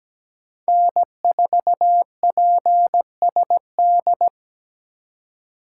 Morse code, heard as a keyed tone, about 20 wpm, 700 Hz. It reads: N4PSD
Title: Speed Racing - USA 1x3 Call Signs 17wpm